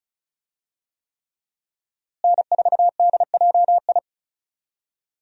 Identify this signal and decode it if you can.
Morse code, heard as a keyed tone, about 36 wpm, 700 Hz. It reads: N4DJI
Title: USA Call Signs - 5 Characters - Repeated 3x - 35wpm